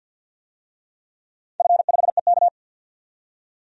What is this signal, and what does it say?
Morse code, heard as a keyed tone, about 51 wpm, 700 Hz. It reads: F5EK
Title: DXCC-100 Call Signs - 4 Characters - 50wpm